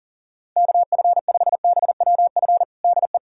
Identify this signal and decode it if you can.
Morse code, heard as a keyed tone, about 40 wpm, 700 Hz. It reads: KF5BWF DE
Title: USA Callsign DE Callsign - 6 Characters - 40wpm